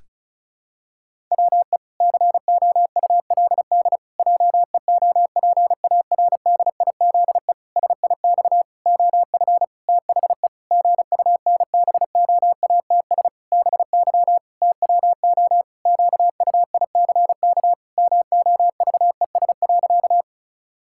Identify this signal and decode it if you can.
Morse code, heard as a keyed tone, about 36 wpm, 700 Hz. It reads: WE COULD JEOPARDIZE SIX OF THE GUNBOATS BY TWO QUICK MOVES.
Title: Pangram Challenge v2 - 35wpm